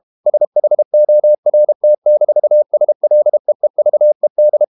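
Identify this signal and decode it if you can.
Morse code, heard as a keyed tone, about 32 wpm, 600 Hz. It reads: SHORT-SLEEVED
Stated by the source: Morse code practice at 32 wpm from CW med Gnister, reading Ken Follett's novel Never in English.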